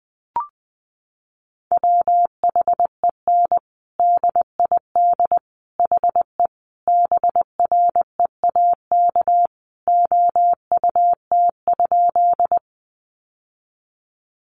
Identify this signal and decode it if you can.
Morse code, heard as a keyed tone, about 20 wpm, 700 Hz. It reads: WHEN DID HE BREAK OUT?
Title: Sentences from Top 900 Words 20wpm